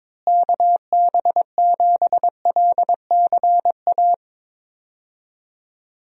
Morse code, heard as a keyed tone, about 22 wpm, 700 Hz. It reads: KB7LCA
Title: USA Call Signs - 6 Characters - Repeated 2x - 22wpm